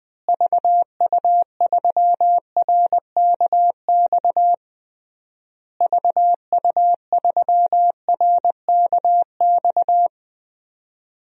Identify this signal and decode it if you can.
Morse code, heard as a keyed tone, about 20 wpm, 700 Hz. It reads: VU3RKX VU3RKX
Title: DXCC-100 Call Signs - 6 Characters - Repeated 3x - 20wpm